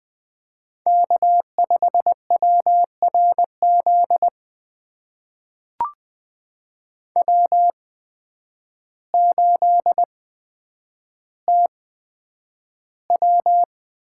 Morse code, heard as a keyed tone, about 20 wpm, spaced as for 9 wpm, 700 Hz. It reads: K5WRZ W8TW
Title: USA Call Signs - 5 Characters - 8x Character Spacing - 20wpm